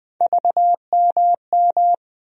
Morse code, heard as a keyed tone, about 21 wpm, 700 Hz. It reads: VMM
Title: USA Call Sign Postfixes - 3 Letters 20wpm